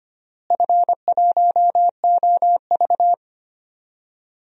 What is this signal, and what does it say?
Morse code, heard as a keyed tone, about 25 wpm, 700 Hz. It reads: F1OV
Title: DXCC-100 Call Signs - 4 Characters - Repeated 3x - 25wpm